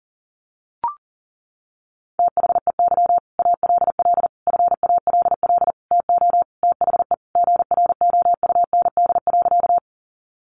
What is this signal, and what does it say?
Morse code, heard as a keyed tone, about 40 wpm, 700 Hz. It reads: THEY ALL FALL TO THE GROUND.
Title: Sentences from Top 500 Words 40wpm